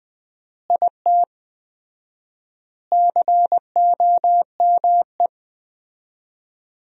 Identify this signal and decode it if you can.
Morse code, heard as a keyed tone, about 20 wpm, 700 Hz. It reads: IT COME
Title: 4x Word-Spacing - Sentences from Top 100 Words - 20wpm